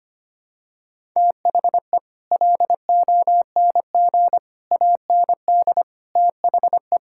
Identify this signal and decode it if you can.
Morse code, heard as a keyed tone, about 25 wpm, 700 Hz. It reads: THE LONG AND THE
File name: Sets of 9 Words - 25wpm